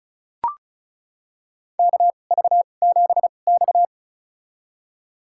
Morse code, heard as a keyed tone, about 34 wpm, spaced as for 25 wpm, 700 Hz. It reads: KV7X
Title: USA Call Signs - 4 Characters - 2x Character Spacing - 35wpm